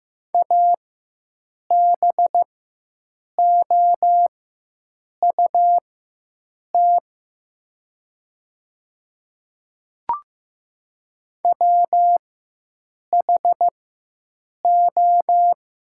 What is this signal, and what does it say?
Morse code, heard as a keyed tone, about 16 wpm, spaced as for 7 wpm, 700 Hz. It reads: ABOUT WHO
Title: 4x Character Spacing - Top 100 Words - 15wpm